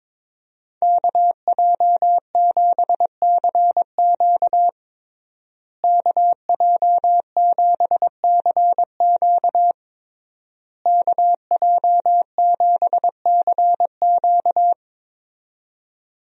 Morse code, heard as a keyed tone, about 22 wpm, 700 Hz. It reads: KJ7CQ KJ7CQ KJ7CQ
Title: USA Call Signs - 5 Characters - Repeated 3x - 22wpm